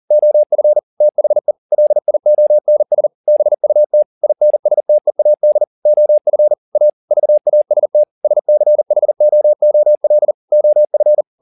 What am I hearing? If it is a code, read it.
Morse code, heard as a keyed tone, about 40 wpm, 600 Hz. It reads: OF THE LIONS BUT INSTEAD OF A VAST SCHOOL OF
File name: the old man and the sea 2 40wpm